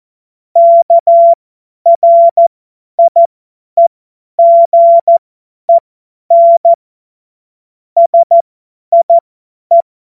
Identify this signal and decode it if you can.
Morse code, heard as a keyed tone, about 14 wpm, spaced as for 10 wpm, 680 Hz. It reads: KRIEGEN SIE
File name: Morsekurs / Witze / Witz 03. Mai 2022 50 F70